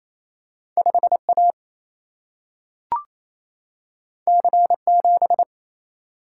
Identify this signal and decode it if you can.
Morse code, heard as a keyed tone, about 28 wpm, 700 Hz. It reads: 5A C7
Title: Single-Letter Single-Number 28wpm